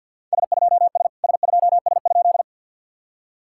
Morse code, heard as a keyed tone, about 53 wpm, 690 Hz. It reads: S2S S2S?
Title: SOTA QSO Practice 50wpm